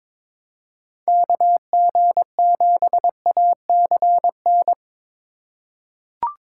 Morse code, heard as a keyed tone, about 22 wpm, 700 Hz. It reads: KG7ACN
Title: USA Call Signs - 6 Characters - 3x Character Spacing - 22wpm